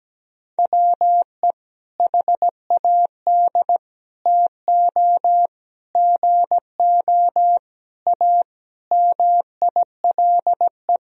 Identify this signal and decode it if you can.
Morse code, heard as a keyed tone, about 17 wpm, 700 Hz. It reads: WE HAD TO GO A MILE
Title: Sentences from Top 1000 Words - Review Part 1 - 17wpm